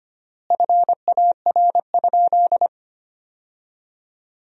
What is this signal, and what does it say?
Morse code, heard as a keyed tone, about 25 wpm, 700 Hz. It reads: FAR?
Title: 3x Word-Spacing - Sentences from Top 300 Words - 25wpm